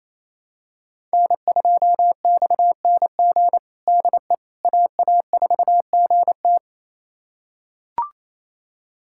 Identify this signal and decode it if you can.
Morse code, heard as a keyed tone, about 28 wpm, 700 Hz. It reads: N2XNG DE AA4GT
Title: USA Callsign DE Callsign - 5 Characters - 28wpm